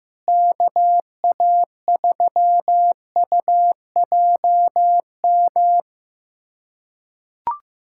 Morse code, heard as a keyed tone, about 16 wpm, 700 Hz. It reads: KA3UJM
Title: USA Callsign DE Callsign - 6 Characters - 15wpm